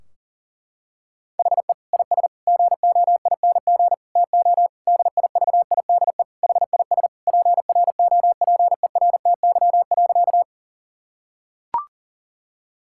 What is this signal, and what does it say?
Morse code, heard as a keyed tone, about 40 wpm, 700 Hz. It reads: HE IS GOING TO DIVIDE HIS PROPERTY.
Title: Sentences from Top 1600 Words 40wpm